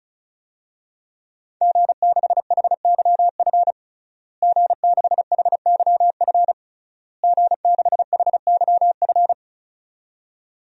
Morse code, heard as a keyed tone, about 36 wpm, 700 Hz. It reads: G6HYF G6HYF G6HYF
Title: DXCC-100 Call Signs - 5 Characters - Repeated 3x - 35wpm